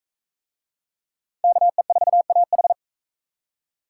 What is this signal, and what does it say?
Morse code, heard as a keyed tone, about 44 wpm, 700 Hz. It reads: KE4AH